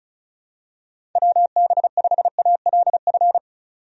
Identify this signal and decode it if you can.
Morse code, heard as a keyed tone, about 36 wpm, 700 Hz. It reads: WB5ALF
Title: USA Call Signs - 6 Characters - 35wpm